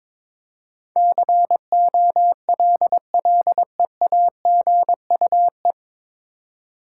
Morse code, heard as a keyed tone, about 22 wpm, 700 Hz. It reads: COLLEAGUE
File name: 9 Letter Words - First 3 Letters Given - 22wpm